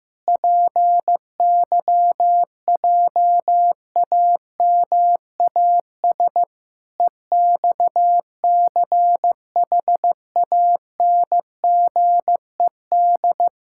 Morse code, heard as a keyed tone, about 15 wpm, 700 Hz. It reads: PYJAMAS EXCHANGED
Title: Pangram Challenge v2 - 15wpm